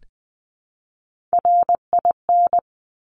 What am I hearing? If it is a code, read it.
Morse code, heard as a keyed tone, about 20 wpm, 700 Hz. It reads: RIN
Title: Top Three Letter Combinations in Words 20wpm